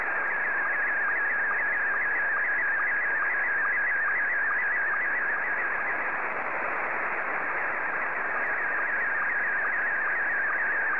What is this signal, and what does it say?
SSTV (slow-scan television), Scottie 1.